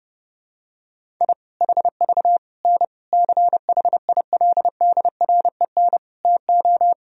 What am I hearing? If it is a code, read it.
Morse code, heard as a keyed tone, about 30 wpm, 700 Hz. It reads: I HV N CHILDREN TO
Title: Sentences from Top 400 Words and Abbreviations 30wpm